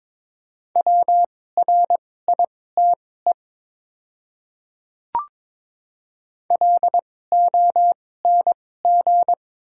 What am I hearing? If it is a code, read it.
Morse code, heard as a keyed tone, about 22 wpm, spaced as for 16 wpm, 700 Hz. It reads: WRITE LONG
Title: 2x Character Spacing - Top 100 Words - 22wpm